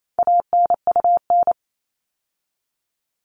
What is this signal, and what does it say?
Morse code, heard as a keyed tone, about 28 wpm, 700 Hz. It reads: ANUN
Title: All Cut Numbers - 4-Digit 28wpm